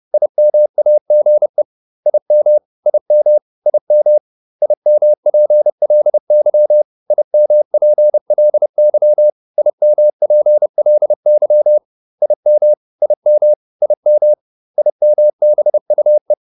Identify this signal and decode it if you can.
Morse code, heard as a keyed tone, about 30 wpm, 600 Hz. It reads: IMAGE IM IM IM IMPLY IMPLY IMPLY IM IM IM IMBUE
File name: IM5 30